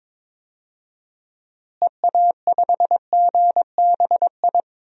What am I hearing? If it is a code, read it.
Morse code, heard as a keyed tone, about 22 wpm, 700 Hz. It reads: EA5GBI
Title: DXCC-100 Call Signs - 6 Characters - Repeated 3x - 22wpm